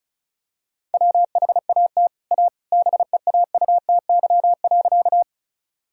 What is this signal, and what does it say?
Morse code, heard as a keyed tone, about 34 wpm, 700 Hz. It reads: WHAT A BEAUTY.
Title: Sentences from Top 1000 Words - Review Part 2 - 35wpm